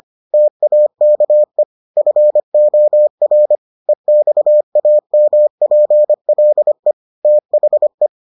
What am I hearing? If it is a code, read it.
Morse code, heard as a keyed tone, about 25 wpm, 600 Hz. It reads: TAKE FOR EXAMPLE THE